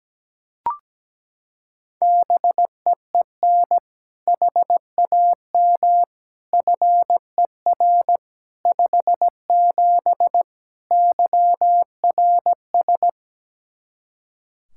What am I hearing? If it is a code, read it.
Morse code, heard as a keyed tone, about 17 wpm, 700 Hz. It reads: BEEN HAM FER 57 YRS
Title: K1USN SST QSO Practice 17wpm